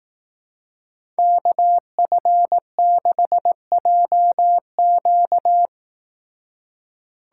Morse code, heard as a keyed tone, about 18 wpm, 700 Hz. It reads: KF6JQ
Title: Speed Racing - USA 2x2 Call Signs 15wpm